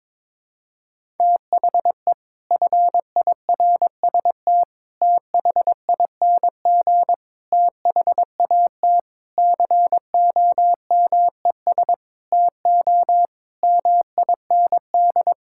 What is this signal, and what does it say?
Morse code, heard as a keyed tone, about 22 wpm, 700 Hz. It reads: THE FIRST THING THAT COMES TO MIND
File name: practice / 22wpm / Sets of 8 Words - 22wpm